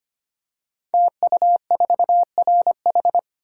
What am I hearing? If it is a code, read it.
Morse code, heard as a keyed tone, about 25 wpm, 700 Hz. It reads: TU4RH
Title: Code Groups - 5 Letter:Numbers - 25wpm